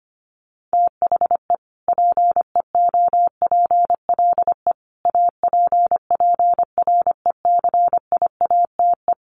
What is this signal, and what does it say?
Morse code, heard as a keyed tone, about 25 wpm, 700 Hz. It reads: THE PEOPLE APPRECIATE
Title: Sentences from Top 1300 Words 25wpm